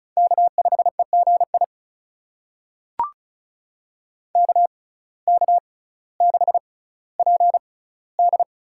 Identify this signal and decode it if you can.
Morse code, heard as a keyed tone, about 36 wpm, spaced as for 18 wpm, 700 Hz. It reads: K5EGI KK6PD
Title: USA Call Signs - 5 Characters - 6x Character Spacing - 35wpm